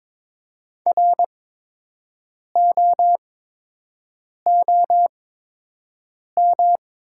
Morse code, heard as a keyed tone, about 22 wpm, spaced as for 6 wpm, 700 Hz. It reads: ROOM
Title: 8x Character Spacing - Top 500 Words - 22wpm